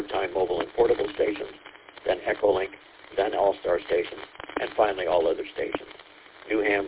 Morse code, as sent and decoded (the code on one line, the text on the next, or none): none